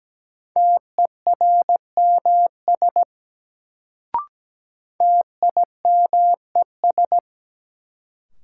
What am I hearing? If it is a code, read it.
Morse code, heard as a keyed tone, about 17 wpm, 700 Hz. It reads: TERMS TIMES
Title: Sets of 3 Words - New Word Drill - 17wpm